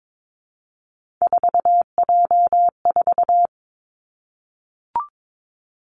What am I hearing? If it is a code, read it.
Morse code, heard as a keyed tone, about 22 wpm, 700 Hz. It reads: 4J4
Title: Three Challenging Letter-Number Combinations 22wpm